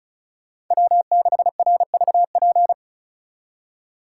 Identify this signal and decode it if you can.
Morse code, heard as a keyed tone, about 43 wpm, 700 Hz. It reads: W6RVP
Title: Speed Racing - USA Call Signs 35wpm